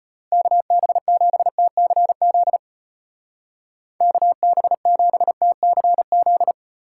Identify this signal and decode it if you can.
Morse code, heard as a keyed tone, about 33 wpm, 700 Hz. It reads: KB7TCZ KB7TCZ